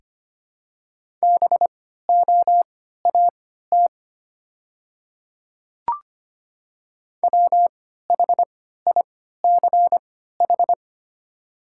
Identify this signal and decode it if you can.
Morse code, heard as a keyed tone, about 25 wpm, spaced as for 14 wpm, 700 Hz. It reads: BOAT WHICH
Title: 3x Character Spacing - Top 500 Words - 25wpm